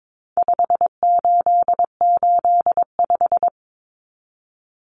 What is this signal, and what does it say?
Morse code, heard as a keyed tone, about 22 wpm, 700 Hz. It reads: 5885
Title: Numbers - 4-Digit 22wpm